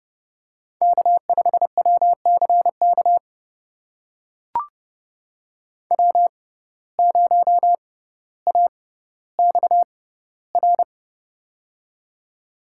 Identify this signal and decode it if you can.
Morse code, heard as a keyed tone, about 32 wpm, spaced as for 16 wpm, 700 Hz. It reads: K5WCK W0AXR
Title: USA Call Signs - 6x Character Spacing - 30wpm